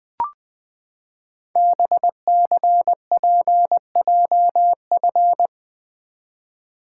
Morse code, heard as a keyed tone, about 20 wpm, 700 Hz. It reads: BCPJF
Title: Code Groups - 5 Letters - 20wpm